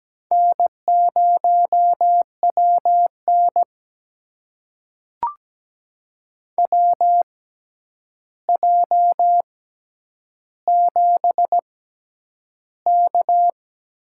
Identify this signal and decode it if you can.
Morse code, heard as a keyed tone, about 17 wpm, spaced as for 9 wpm, 700 Hz. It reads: N0WN WJ7K
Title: USA Call Signs - 4 Characters - 6x Character Spacing - 17wpm